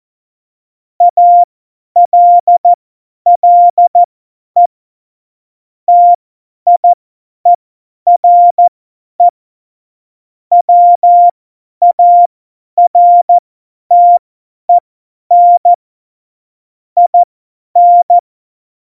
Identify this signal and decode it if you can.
Morse code, heard as a keyed tone, about 14 wpm, spaced as for 10 wpm, 700 Hz. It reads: ALLE TIERE WARTEN IN